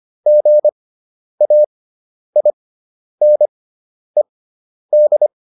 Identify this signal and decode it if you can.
Morse code, heard as a keyed tone, about 25 wpm, spaced as for 10 wpm, 600 Hz. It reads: GAINED